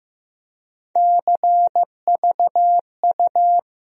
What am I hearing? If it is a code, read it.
Morse code, heard as a keyed tone, about 15 wpm, 700 Hz. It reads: CVU
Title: USA Call Sign Postfixes - 3 Letters 15wpm